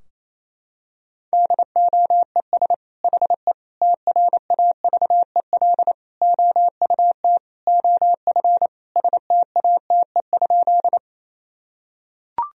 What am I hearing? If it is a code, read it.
Morse code, heard as a keyed tone, about 28 wpm, 700 Hz. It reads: DOES HE TRAVEL OUT OF STATE?
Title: Sentences from Top 1000 Words - Review Part 2 - 28wpm